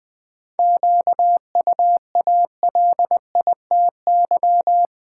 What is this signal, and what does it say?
Morse code, heard as a keyed tone, about 20 wpm, 700 Hz. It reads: QUALITY
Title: Sentences from Top 1000 Words - Review Part 4 - New Word Drill - 20wpm